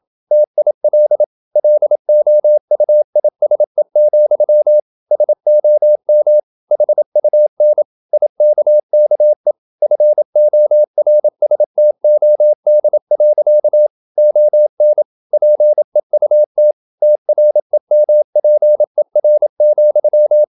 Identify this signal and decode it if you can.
Morse code, heard as a keyed tone, about 27 wpm, 600 Hz. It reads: TIL LOUISE, SOM HUN IKKE FORSTOD. ON PEUT TREMPER,
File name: Ferie QRQ 27wpm